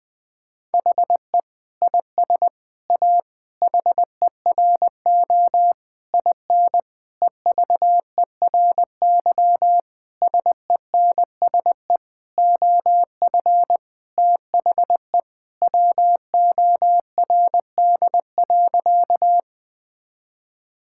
Morse code, heard as a keyed tone, about 20 wpm, 700 Hz. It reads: HE IS A HERO IN EVERY SENSE OF THE WORD.
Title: Sentences with 6-Word Phrases - 20wpm